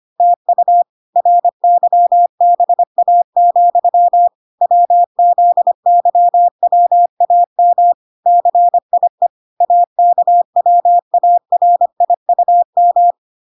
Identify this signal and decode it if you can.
Morse code, heard as a keyed tone, about 25 wpm, 700 Hz. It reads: TU RYBA, WZYWAM CIE AKWARIUM